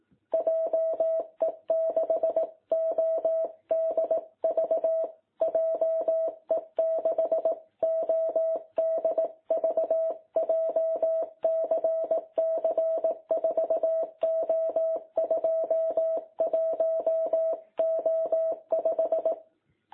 Morse code, heard as a keyed tone, about 18 wpm, 660 Hz. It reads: JE6ODV JE6ODV JCC 4O21O5
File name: jh8cbh / 2021cw